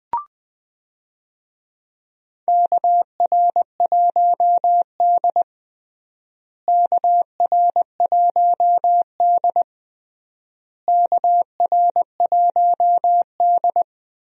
Morse code, heard as a keyed tone, about 20 wpm, 700 Hz. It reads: KR1D KR1D KR1D
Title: USA Call Signs - 4 Characters - Repeated 3x - 20wpm